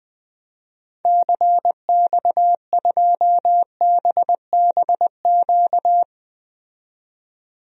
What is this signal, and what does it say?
Morse code, heard as a keyed tone, about 20 wpm, 700 Hz. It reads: CX2BBQ
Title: DXCC-100 Call Signs - 6 Characters - 20wpm